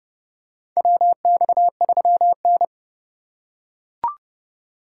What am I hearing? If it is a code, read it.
Morse code, heard as a keyed tone, about 30 wpm, 700 Hz. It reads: WX3N
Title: USA Call Signs - 3x Character Spacing - 30wpm